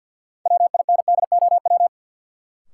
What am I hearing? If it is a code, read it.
Morse code, heard as a keyed tone, about 52 wpm, 700 Hz. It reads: WINDOW
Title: Sentences from Top 1600 Words - New Word Drill - 50wpm